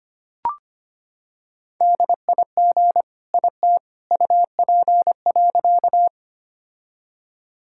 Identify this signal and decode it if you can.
Morse code, heard as a keyed tone, about 25 wpm, 700 Hz. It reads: DIG IT UP.
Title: Sentences from Top 1600 Words 25wpm